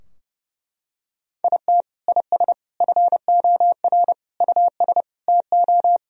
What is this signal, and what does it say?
Morse code, heard as a keyed tone, about 30 wpm, 700 Hz. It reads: IT IS FOR US TO
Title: Sentences from Top 1000 Words - Review Part 4 - 30wpm